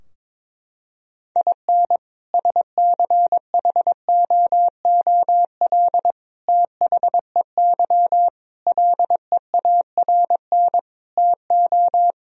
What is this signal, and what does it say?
Morse code, heard as a keyed tone, about 22 wpm, 700 Hz. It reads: IN SCHOOL THEY LEARN TO
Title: Sentences from Top 1000 Words - Review Part 4 - 22wpm